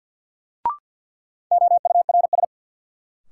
Morse code, heard as a keyed tone, about 48 wpm, 700 Hz. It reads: OURS
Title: Sentences from Top 1300 Words - New Word Drill Alt - 50wpm